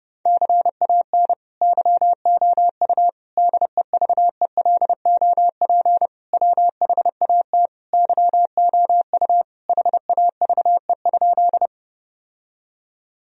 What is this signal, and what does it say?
Morse code, heard as a keyed tone, about 30 wpm, 700 Hz. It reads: CAN YOU DEVELOP WHAT YOU HAVE?
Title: Sentences from Top 1000 Words - Review Part 2 - 30wpm